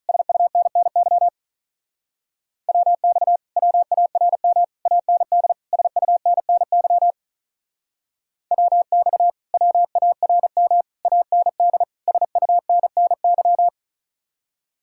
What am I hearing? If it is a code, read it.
Morse code, heard as a keyed tone, about 46 wpm, 700 Hz. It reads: SUNNY WX WARM AND SUNNY WX WARM AND SUNNY